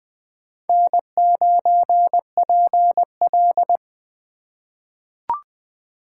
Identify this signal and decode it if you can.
Morse code, heard as a keyed tone, about 20 wpm, 700 Hz. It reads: N9PL
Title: USA Call Signs - 4 Characters - 3x Character Spacing - 20wpm